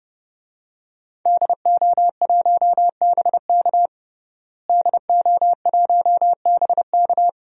Morse code, heard as a keyed tone, about 30 wpm, 700 Hz. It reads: DO1BK DO1BK
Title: DXCC-100 Call Signs - 5 Characters - Repeated 3x - 30wpm